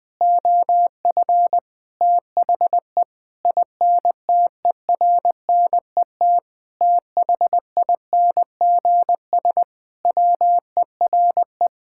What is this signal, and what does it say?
Morse code, heard as a keyed tone, about 21 wpm, 700 Hz. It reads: OF THE INTERNET THINGS WERE
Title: Sentences with 6-Word Phrases - 20wpm